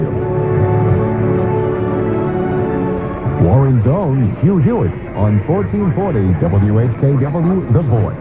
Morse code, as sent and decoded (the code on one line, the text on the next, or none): none